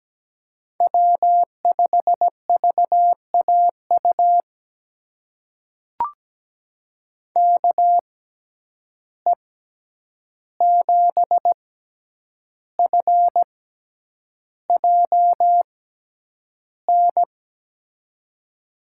Morse code, heard as a keyed tone, about 17 wpm, spaced as for 8 wpm, 700 Hz. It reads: W5VAU KE7FJN
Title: USA Call Signs - 6x Character Spacing - 17wpm